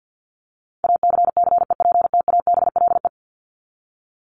Morse code, heard as a keyed tone, about 50 wpm, 700 Hz. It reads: ACCEPTABLE
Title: Top 1501-1750 Words 50wpm